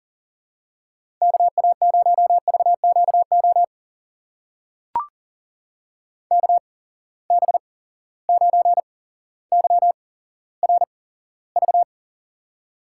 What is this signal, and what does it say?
Morse code, heard as a keyed tone, about 40 wpm, spaced as for 17 wpm, 700 Hz. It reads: KA0VQO KB9YRV
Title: USA Call Signs - 6 Characters - 8x Character Spacing - 40wpm